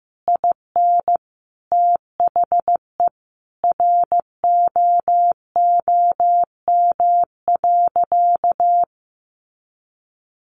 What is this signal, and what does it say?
Morse code, heard as a keyed tone, about 15 wpm, 700 Hz. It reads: IN THE ROOM.
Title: A Sound Of Thunder 15wpm